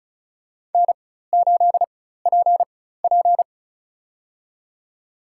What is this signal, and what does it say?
Morse code, heard as a keyed tone, about 36 wpm, spaced as for 16 wpm, 700 Hz. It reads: N8PP
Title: USA Call Signs - 4 Characters - 4x Character Spacing - 35wpm